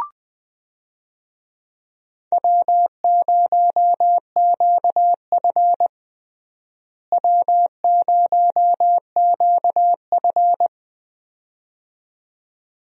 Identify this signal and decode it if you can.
Morse code, heard as a keyed tone, about 20 wpm, 700 Hz. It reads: W0QF W0QF
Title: USA Call Signs - 4 Characters - Repeated 2x - 20wpm